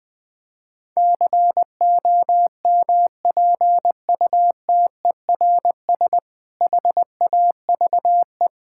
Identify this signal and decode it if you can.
Morse code, heard as a keyed tone, about 20 wpm, 700 Hz. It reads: COMPUTERS HAVE
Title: Difficult Sentences with Hints - 20wpm